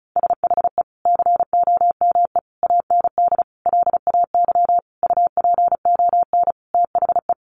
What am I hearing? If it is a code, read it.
Morse code, heard as a keyed tone, about 34 wpm, 700 Hz. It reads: SHE COME AND LAY UPON THE
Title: Sentences from Top 1100 Words 35wpm